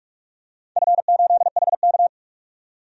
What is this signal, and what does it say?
Morse code, heard as a keyed tone, about 44 wpm, 700 Hz. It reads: F8HK
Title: DXCC-100 Call Signs - 4 Characters - 45wpm